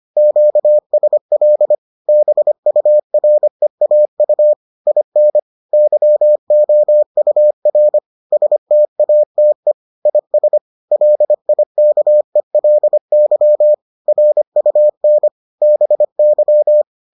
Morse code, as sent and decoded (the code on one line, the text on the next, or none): --.- ... .-.. / -... ..- .-. .- ..- / .. -. / -.-- --- ..- .-. / ... - .- - . / .. ... / .-.. .. -.- . .-.. -.-- / .-. ..- -. / -... -.--
QSL BUREAU IN YOUR STATE IS LIKELY RUN BY